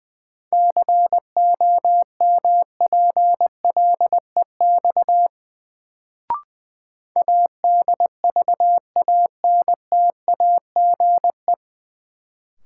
Morse code, heard as a keyed tone, about 20 wpm, 700 Hz. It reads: COMPLEX ADVANTAGE
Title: Sentences from Top 900 Words - New Word Drill - 20wpm